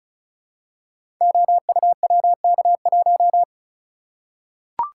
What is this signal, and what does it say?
Morse code, heard as a keyed tone, about 36 wpm, 700 Hz. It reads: OUWK1
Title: Code Groups - 5 Letter:Numbers - 35wpm